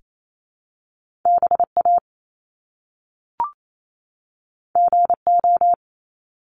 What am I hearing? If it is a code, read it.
Morse code, heard as a keyed tone, about 28 wpm, 700 Hz. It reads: BA GO
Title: Top Two Letter Combinations in Words 28wpm